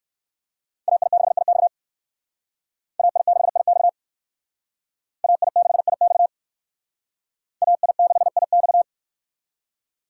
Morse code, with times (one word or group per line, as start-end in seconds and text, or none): 0.9-1.7 AI6IX
3.0-3.9 AI6IX
5.2-6.3 AI6IX
7.6-8.8 AI6IX